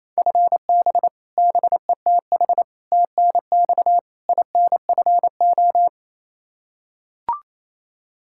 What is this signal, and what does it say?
Morse code, heard as a keyed tone, about 27 wpm, 700 Hz. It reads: FB BETH TNX INFO
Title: Practice QSO - Exchange 3 - More Info - 28wpm